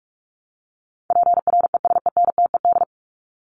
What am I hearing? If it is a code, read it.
Morse code, heard as a keyed tone, about 44 wpm, 700 Hz. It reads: PRESENTED
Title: ED Suffix - Top 100 Words with Base Words 45wpm